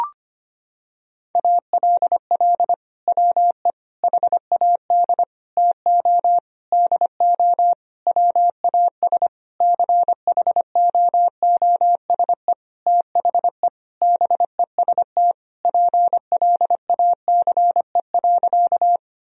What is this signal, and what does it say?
Morse code, heard as a keyed tone, about 25 wpm, 700 Hz. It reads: ALL WE HAD TO DO WAS CHOOSE THE BEST PLACE.
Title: Sentences from Top 1000 Words - Review Part 4 - 25wpm